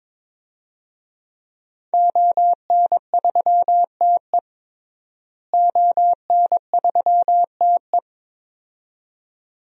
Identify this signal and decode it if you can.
Morse code, heard as a keyed tone, about 22 wpm, 700 Hz. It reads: ON3TE ON3TE
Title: DXCC-100 Call Signs - 5 Characters - Repeated 2x - 22wpm